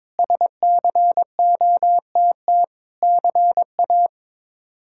Morse code, heard as a keyed tone, about 22 wpm, 700 Hz. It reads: SCOTT CA
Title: K1USN SST QSO Practice - 22wpm